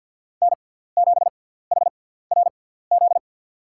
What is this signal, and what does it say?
Morse code, heard as a keyed tone, about 53 wpm, spaced as for 17 wpm, 700 Hz. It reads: N7HRZ
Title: USA Call Signs - 5 Characters - 6x Character Spacing - 50wpm